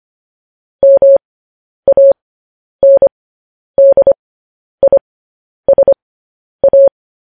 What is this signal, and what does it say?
Morse code, heard as a keyed tone, about 25 wpm, spaced as for 10 wpm, 550 Hz. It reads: MANDISA